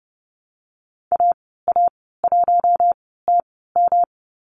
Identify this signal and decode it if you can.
Morse code, heard as a keyed tone, about 30 wpm, spaced as for 17 wpm, 700 Hz. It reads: AA1TM
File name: Lesson 19g - Letter w - Call Signs 30wpm